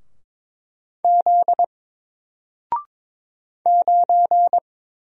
Morse code, heard as a keyed tone, about 22 wpm, 700 Hz. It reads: Z 9
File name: Single-Letter-Number-22wpm